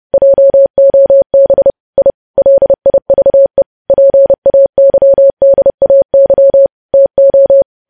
Morse code, heard as a keyed tone, about 30 wpm, 550 Hz. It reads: JOB I LIVE PAYDAY TO
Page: news headlines in morse code at 30 wpm